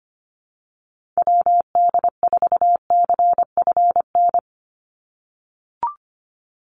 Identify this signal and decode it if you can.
Morse code, heard as a keyed tone, about 25 wpm, 700 Hz. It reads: WD4CFN
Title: NA SOTA Chaser Callsigns 25wpm